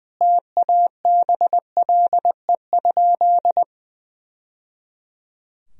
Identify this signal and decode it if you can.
Morse code, heard as a keyed tone, about 20 wpm, 700 Hz. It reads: TABLE?
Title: Sentences from Top 1000 Words - Review Part 3 - 20wpm